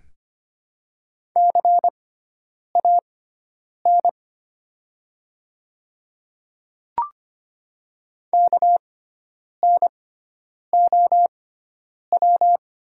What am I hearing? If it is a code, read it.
Morse code, heard as a keyed tone, about 25 wpm, spaced as for 9 wpm, 700 Hz. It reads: CAN KNOW
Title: 6x Character Spacing - Top 100 Words - 25wpm